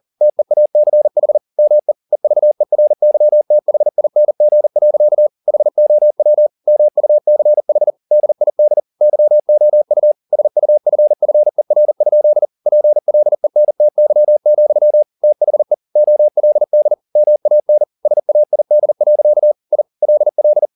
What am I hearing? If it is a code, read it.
Morse code, heard as a keyed tone, about 40 wpm, 600 Hz. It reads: TEACH ME EVERYTHING. HOW MUCH DID YOU SUFFER? PLENTY, THE OLD MAN SAID. I LL
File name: the old man and the sea 2 40wpm